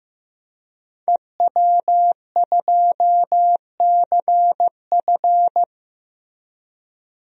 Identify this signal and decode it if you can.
Morse code, heard as a keyed tone, about 15 wpm, 700 Hz. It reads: EW2CF
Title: DXCC-100 Call Signs - 5 Characters - Repeated 3x - 15wpm